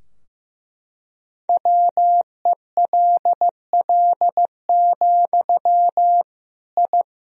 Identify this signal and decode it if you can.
Morse code, heard as a keyed tone, about 16 wpm, 700 Hz. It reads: WELL, I
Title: Sentences from Top 1000 Words - Review Part 2 - 15wpm